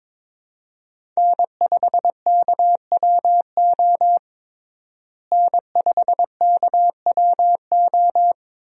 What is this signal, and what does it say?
Morse code, heard as a keyed tone, about 22 wpm, 700 Hz. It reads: N5KWO N5KWO
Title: USA Call Signs - 5 Characters - Repeated 3x - 22wpm